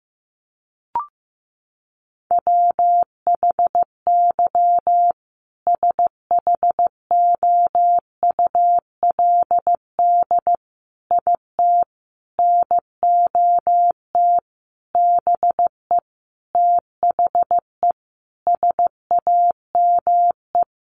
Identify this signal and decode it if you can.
Morse code, heard as a keyed tone, about 15 wpm, 700 Hz. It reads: WHY SHOULD IT NOT BE THE SAME
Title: Sentences from Top 1400 Words 15wpm